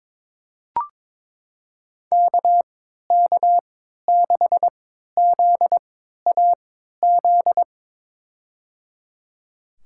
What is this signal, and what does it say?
Morse code, heard as a keyed tone, about 22 wpm, spaced as for 13 wpm, 700 Hz. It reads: KK6ZAZ
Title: USA Call Signs - 6 Characters - 3x Character Spacing - 22wpm